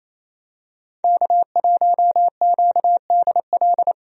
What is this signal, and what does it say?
Morse code, heard as a keyed tone, about 28 wpm, 700 Hz. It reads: K1QDL
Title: USA Call Signs - 5 Characters - Repeated 3x - 28wpm